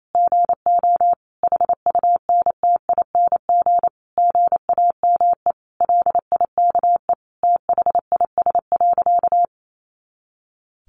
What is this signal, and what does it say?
Morse code, heard as a keyed tone, about 28 wpm, 700 Hz. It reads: GO HUNTING GAME LIKE THIS.
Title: A Sound Of Thunder 28wpm